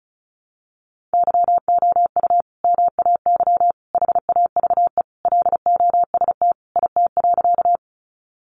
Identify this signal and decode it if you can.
Morse code, heard as a keyed tone, about 34 wpm, 700 Hz. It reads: YOU MAY HAVE LOST IT.
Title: Sentences from Top 1400 Words 35wpm